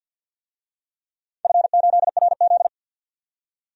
Morse code, heard as a keyed tone, about 52 wpm, 700 Hz. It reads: F8RZ
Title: DXCC-100 Call Signs - 4 Characters - 50wpm